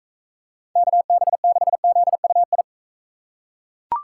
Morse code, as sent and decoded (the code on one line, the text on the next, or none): -.- -... -.... --.. ..- ..
KB6ZUI